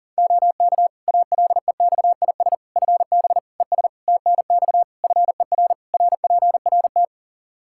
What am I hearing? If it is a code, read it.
Morse code, heard as a keyed tone, about 40 wpm, 700 Hz. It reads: OK ALEXIS FB ES TNX FER RPRT
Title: Practice QSO - Exchange 2 - Info - 40wpm